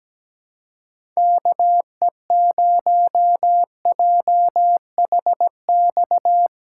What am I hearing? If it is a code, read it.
Morse code, heard as a keyed tone, about 17 wpm, 700 Hz. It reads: KE0JHX
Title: USA Callsign DE Callsign - 6 Characters - 17wpm